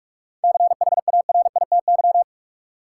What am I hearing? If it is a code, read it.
Morse code, heard as a keyed tone, about 42 wpm, 700 Hz. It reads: CHARITY
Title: Seven Letter Words - 45wpm